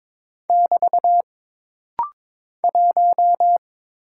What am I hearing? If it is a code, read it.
Morse code, heard as a keyed tone, about 22 wpm, 700 Hz. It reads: <BT> 1
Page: This rapid-fire timing is geared to push your ICR (Instant Character Recognition) skills.